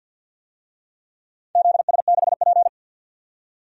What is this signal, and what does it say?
Morse code, heard as a keyed tone, about 47 wpm, 700 Hz. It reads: ZS6P
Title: DXCC-100 Call Signs - 4 Characters - 50wpm